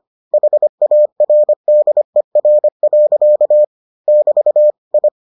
Morse code, heard as a keyed tone, about 25 wpm, 600 Hz. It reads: HARDER. <BT> I